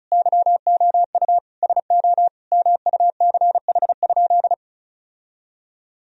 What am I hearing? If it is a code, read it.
Morse code, heard as a keyed tone, about 34 wpm, 700 Hz. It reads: YOU SO MUCH?
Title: Sentences from Top 1000 Words - Review Part 2 - 35wpm